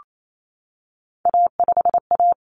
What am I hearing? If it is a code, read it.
Morse code, heard as a keyed tone, about 28 wpm, 700 Hz. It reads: A5A
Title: Common Cut Numbers - 3-Digit 28wpm